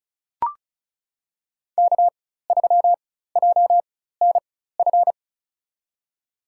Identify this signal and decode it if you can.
Morse code, heard as a keyed tone, about 34 wpm, spaced as for 16 wpm, 700 Hz. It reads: K3JNF
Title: USA Call Signs - 5 Characters - 4x Character Spacing - 35wpm